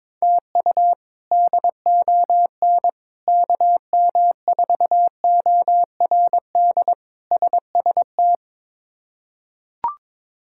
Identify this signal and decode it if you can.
Morse code, heard as a keyed tone, about 22 wpm, 700 Hz. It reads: TU DON KM4ORD SST
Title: K1USN SST QSO Practice - 22wpm